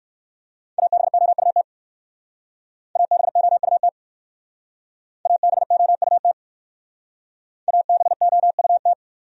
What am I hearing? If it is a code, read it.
Morse code, heard as a keyed tone, about 55 wpm, 700 Hz. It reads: ABOUT ABOUT ABOUT ABOUT
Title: Speed-Racing Top 200 Words 45wpm